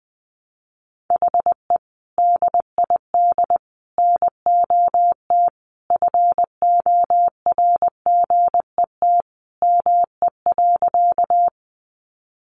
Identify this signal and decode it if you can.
Morse code, heard as a keyed tone, about 20 wpm, 700 Hz. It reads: HE DID NOT FORGET ME.
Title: Sentences from Top 1200 Words 20wpm